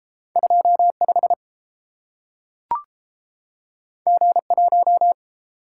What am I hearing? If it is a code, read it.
Morse code, heard as a keyed tone, about 32 wpm, 700 Hz. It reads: 25 G1